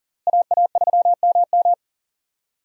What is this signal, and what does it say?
Morse code, heard as a keyed tone, about 40 wpm, 700 Hz. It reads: AA3MM
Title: USA Call Signs - 5 Characters - Repeated 3x - 40wpm